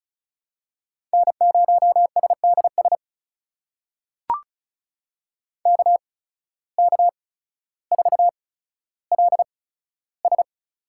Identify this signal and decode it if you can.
Morse code, heard as a keyed tone, about 36 wpm, spaced as for 15 wpm, 700 Hz. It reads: N0SDS KK4LS
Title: USA Call Signs - 5 Characters - 8x Character Spacing - 35wpm